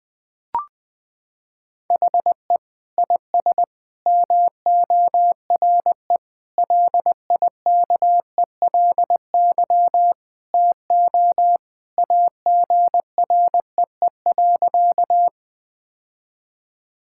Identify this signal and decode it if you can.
Morse code, heard as a keyed tone, about 20 wpm, 700 Hz. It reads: HE IS MORE LIKELY TO AGREE.
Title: Sentences with 3-Word Phrases - 20wpm